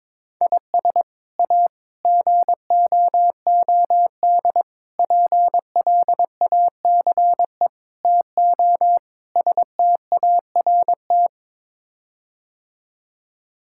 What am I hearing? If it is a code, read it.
Morse code, heard as a keyed tone, about 22 wpm, 700 Hz. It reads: IS A GOOD PLACE TO START
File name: practice / 22wpm / Sets of 6 Words - 22wpm